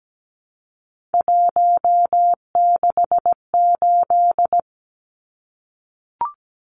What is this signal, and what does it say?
Morse code, heard as a keyed tone, about 17 wpm, 700 Hz. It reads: 168
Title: Numbers - 3-Digit 17wpm